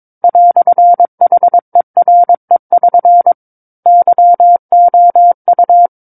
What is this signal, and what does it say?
Morse code, heard as a keyed tone, about 22 wpm, 700 Hz. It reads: "HERE<SN> YOU
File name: 06-Chapter-06 22 WPM